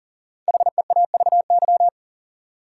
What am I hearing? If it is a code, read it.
Morse code, heard as a keyed tone, about 40 wpm, 700 Hz. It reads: HEAVY
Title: Sentences from Top 1000 Words - Review Part 6 - New Word Drill - 40wpm